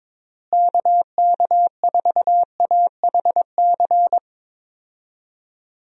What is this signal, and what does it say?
Morse code, heard as a keyed tone, about 22 wpm, 700 Hz. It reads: KK4AHC
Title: USA Call Signs - 6 Characters - Repeated 3x - 22wpm